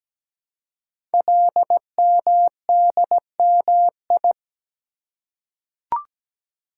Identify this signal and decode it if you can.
Morse code, heard as a keyed tone, about 17 wpm, 700 Hz. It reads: LMDMI
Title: Code Groups - 5 Letters - 17wpm